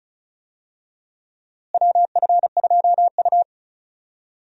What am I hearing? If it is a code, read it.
Morse code, heard as a keyed tone, about 36 wpm, 700 Hz. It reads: WF2U
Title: USA Call Signs - 4 Characters - Repeated 3x - 35wpm